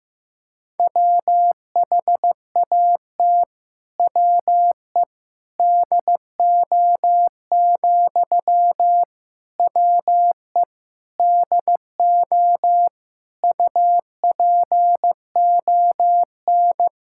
Morse code, heard as a keyed tone, about 15 wpm, 700 Hz. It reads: WHAT WE DO, WE DO UPON
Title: Sentences from Top 1600 Words 15wpm